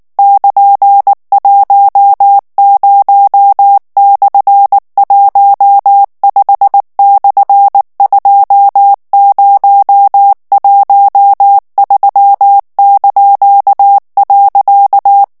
Morse code, heard as a keyed tone, about 19 wpm, 800 Hz. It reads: <KN>10/15/2013).
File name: morse mp3 / ANGLAIS texte clair 3 vitesse 18 mots minute